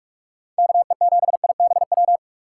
Morse code, heard as a keyed tone, about 44 wpm, 700 Hz. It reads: KE7IBW